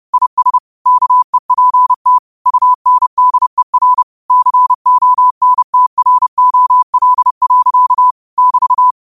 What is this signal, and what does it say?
Morse code, heard as a keyed tone, about 30 wpm, 1000 Hz. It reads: IS KEPT UNDER CONTROL. <BT>
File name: qotdfiles / 30wpm